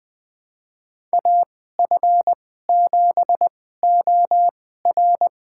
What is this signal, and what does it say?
Morse code, heard as a keyed tone, about 21 wpm, spaced as for 15 wpm, 700 Hz. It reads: AF7OR
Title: USA Call Signs - 5 Characters - 2x Character Spacing - 20wpm